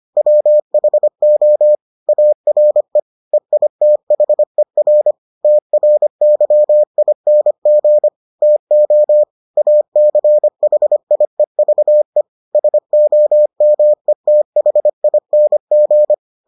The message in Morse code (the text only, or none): WHO ARE EITHER TRYING TO ACHIEVE SOMETHING